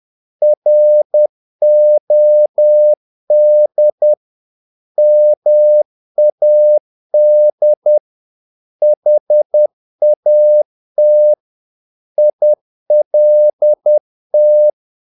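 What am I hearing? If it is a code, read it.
Morse code, heard as a keyed tone, about 10 wpm, 600 Hz. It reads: ROD MAD HAT ILT
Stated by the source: Morse practice of short Danish words at 10 wpm, from CW med Gnister.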